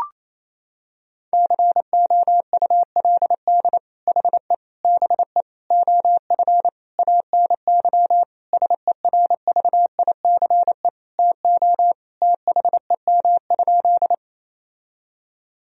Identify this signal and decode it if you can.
Morse code, heard as a keyed tone, about 28 wpm, 700 Hz. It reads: COULD HE BE OF ANY SERVICE TO THEM?
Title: Sentences from Top 1000 Words - Review Part 4 - 28wpm